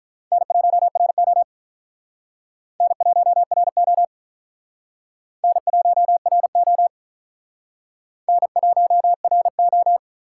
Morse code, heard as a keyed tone, about 47 wpm, 700 Hz. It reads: N1RO N1RO N1RO N1RO